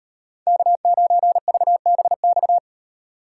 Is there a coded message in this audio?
Yes: Morse code from 0.5 to 2.6 s, heard as a keyed tone, about 38 wpm, 700 Hz. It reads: K9VBX